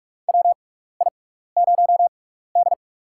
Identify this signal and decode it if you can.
Morse code, heard as a keyed tone, about 42 wpm, spaced as for 15 wpm, 700 Hz. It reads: WI0D
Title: USA Call Signs - 4 Characters - 6x Character Spacing - 45wpm